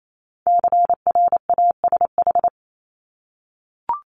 Morse code, heard as a keyed tone, about 28 wpm, 700 Hz. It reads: CRASH
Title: Multi-letter Phonemes Repeated 3x 28wpm